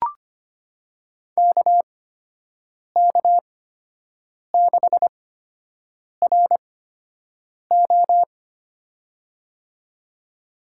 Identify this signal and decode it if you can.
Morse code, heard as a keyed tone, about 25 wpm, spaced as for 7 wpm, 700 Hz. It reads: KK6RO
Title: USA Call Signs - 5 Characters - 8x Character Spacing - 25wpm